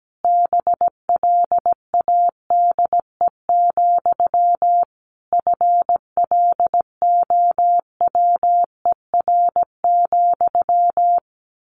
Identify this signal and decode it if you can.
Morse code, heard as a keyed tone, about 17 wpm, 700 Hz. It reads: BLADE, FLOWER,
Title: A Sound Of Thunder 17wpm